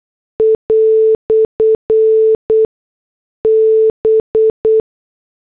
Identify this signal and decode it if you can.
Morse code, heard as a keyed tone, about 8 wpm, spaced as for 6 wpm, 440 Hz. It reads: "B